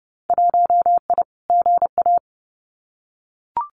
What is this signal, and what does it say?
Morse code, heard as a keyed tone, about 30 wpm, 700 Hz. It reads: 1I GA
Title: Winter Field Day Exchanges Encore 30wpm